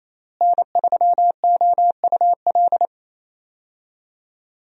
Morse code, heard as a keyed tone, about 28 wpm, 700 Hz. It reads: N3OUL
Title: USA Call Signs - 5 Characters - Repeated 3x - 28wpm